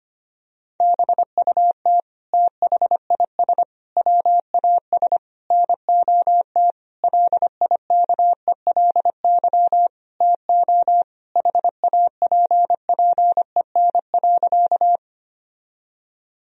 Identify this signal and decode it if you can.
Morse code, heard as a keyed tone, about 25 wpm, 700 Hz. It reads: BUT THIS WAS NOT LIKELY TO HAPPEN.
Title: Sentences from Top 1000 Words - Review Part 4 - 25wpm